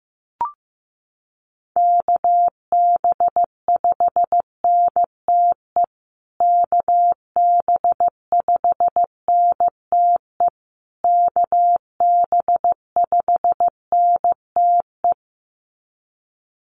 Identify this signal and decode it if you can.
Morse code, heard as a keyed tone, about 15 wpm, 700 Hz. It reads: KB5NTE KB5NTE KB5NTE
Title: USA Call Signs Repeated 3x 15wpm